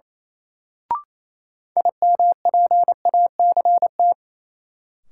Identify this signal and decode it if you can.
Morse code, heard as a keyed tone, about 28 wpm, 700 Hz. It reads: IMPACT
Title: Sentences from Top 1000 Words - Review Part 6 - New Word Drill - 28wpm